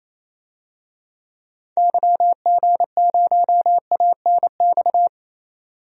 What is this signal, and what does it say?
Morse code, heard as a keyed tone, about 28 wpm, 700 Hz. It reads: YG0ANX
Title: DXCC-100 Call Signs - 6 Characters - 28wpm